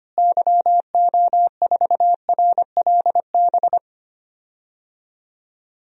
Morse code, heard as a keyed tone, about 25 wpm, 700 Hz. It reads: YO4RLB
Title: DXCC-100 Call Signs - 6 Characters - Repeated 2x - 25wpm